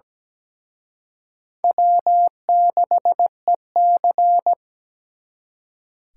Morse code, heard as a keyed tone, about 20 wpm, 700 Hz. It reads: W6EC
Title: Speed Racing - USA 1x2 Call Signs 17wpm